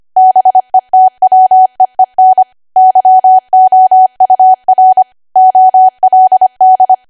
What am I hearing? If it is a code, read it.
Morse code, heard as a keyed tone, about 25 wpm, 750 Hz. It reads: BETWEEN YOUR OLD
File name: Archive / 25 WPM / 25WPM